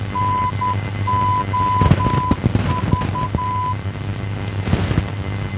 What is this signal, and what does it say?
Morse code, heard as a keyed tone, about 10 wpm, 1000 Hz, with noise. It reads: NOV
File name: NOV NDB 1988